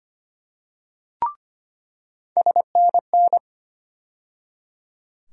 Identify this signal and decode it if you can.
Morse code, heard as a keyed tone, about 25 wpm, 700 Hz. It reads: SNN
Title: USA Call Sign Postfixes - 3 Letters 25wpm